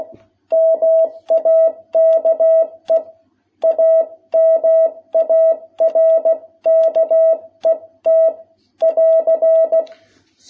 Morse code, heard as a keyed tone, about 16 wpm, 660 Hz. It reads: MAKE A MARKET <AR>
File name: jh8cbh / 2021cw